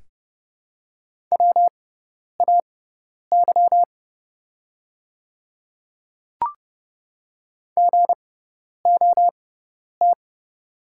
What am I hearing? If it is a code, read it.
Morse code, heard as a keyed tone, about 30 wpm, spaced as for 10 wpm, 700 Hz. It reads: WAY GOT
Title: 6x Character Spacing - Top 300 Words - 30wpm